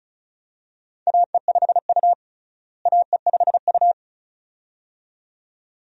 Morse code, heard as a keyed tone, about 36 wpm, 700 Hz. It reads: AE5U AE5U
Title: USA Call Signs - 4 Characters - Repeated 2x - 35wpm